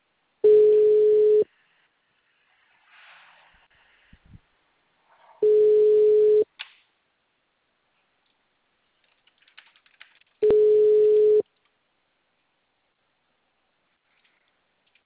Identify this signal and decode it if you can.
Ringback tone (Europe, 425 Hz).